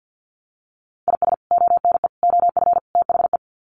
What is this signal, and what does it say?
Morse code, heard as a keyed tone, about 50 wpm, 700 Hz. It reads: IS ONE OF THE
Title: Sets of 4 Words 50wpm